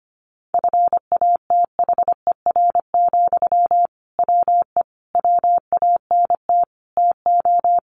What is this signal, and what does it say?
Morse code, heard as a keyed tone, about 25 wpm, 700 Hz. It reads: FATHER, WE WANT TO
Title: Sentences from Top 700 Words 25wpm